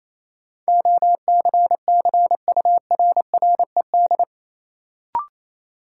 Morse code, heard as a keyed tone, about 28 wpm, 700 Hz. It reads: OCCURRED
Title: Sets of 5 Words Encore - New Word Drill - 28wpm